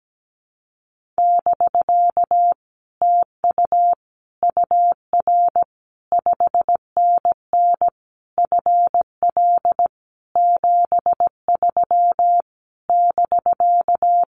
Morse code, heard as a keyed tone, about 17 wpm, 700 Hz. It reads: <BK> TU UR 5NN FL 73 <BK>
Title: Common QSO Phrases Encore 17wpm